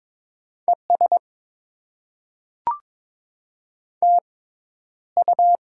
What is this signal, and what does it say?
Morse code, heard as a keyed tone, about 22 wpm, spaced as for 11 wpm, 690 Hz. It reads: ES TU
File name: Top 20 QSO Elements - 6x Character Spacing - 22wpm